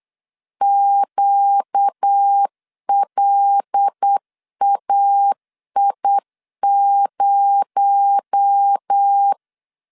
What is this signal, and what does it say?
Morse code, heard as a keyed tone, about 8 wpm, 800 Hz. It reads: QLAI0
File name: LEZIONI / 6 PKX0 / 6 PKX0 8wpm 3341